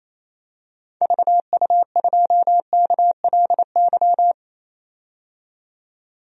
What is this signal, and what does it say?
Morse code, heard as a keyed tone, about 28 wpm, 700 Hz. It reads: VU2KLY
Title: DXCC-100 Call Signs - 6 Characters - 28wpm